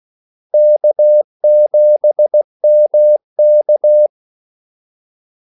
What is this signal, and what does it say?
Morse code, heard as a keyed tone, about 16 wpm, 600 Hz. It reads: K7MK
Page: Kaldesignaler 16 wpm